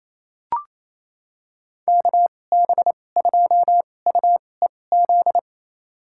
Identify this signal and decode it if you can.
Morse code, heard as a keyed tone, about 28 wpm, spaced as for 20 wpm, 700 Hz. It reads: KB2UEZ
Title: USA Call Signs - 6 Characters - 2x Character Spacing - 28wpm